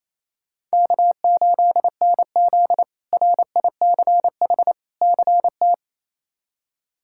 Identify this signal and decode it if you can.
Morse code, heard as a keyed tone, about 28 wpm, 700 Hz. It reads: K8NZ RICH CT
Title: K1USN SST QSO Practice - 28wpm